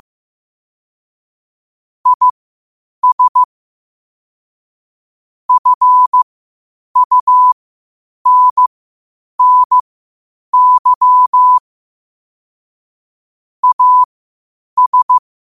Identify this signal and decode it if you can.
Morse code, heard as a keyed tone, about 15 wpm, spaced as for 8 wpm, 1000 Hz. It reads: IS FUNNY AS
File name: 8wpm